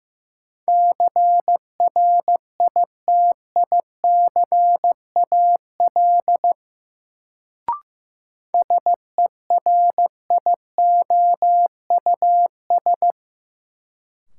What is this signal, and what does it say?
Morse code, heard as a keyed tone, about 15 wpm, 700 Hz. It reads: CRITICAL SERIOUS
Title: Sentences from Top 1100 Words - New Word Drill Alt - 15wpm